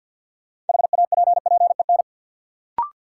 Morse code, heard as a keyed tone, about 48 wpm, 700 Hz. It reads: HAPPEN
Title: Sets of 5 Words Encore - New Word Drill - 50wpm